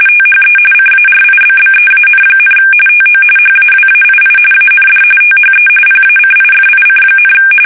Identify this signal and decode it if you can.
RTTY, about 45.45 baud. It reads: THE CARIBBEAN, TO HAWAII AND THE PACIFIC,
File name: fsk850 / usnavy3